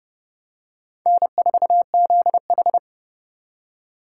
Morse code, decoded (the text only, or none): N4ZH